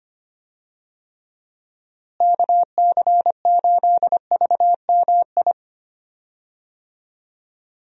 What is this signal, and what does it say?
Morse code, heard as a keyed tone, about 25 wpm, 700 Hz. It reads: KC8VMI
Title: USA Call Signs - 6 Characters - Repeated 3x - 25wpm